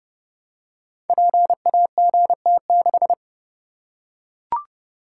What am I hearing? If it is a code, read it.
Morse code, heard as a keyed tone, about 28 wpm, 700 Hz. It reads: PAGT6
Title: Code Groups - 5 Letter:Numbers - 30wpm